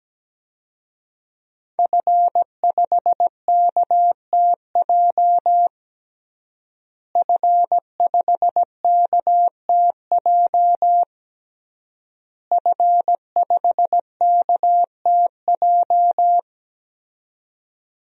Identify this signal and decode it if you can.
Morse code, heard as a keyed tone, about 17 wpm, 700 Hz. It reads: F5KTJ F5KTJ F5KTJ
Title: DXCC-100 Call Signs - 5 Characters - Repeated 3x - 17wpm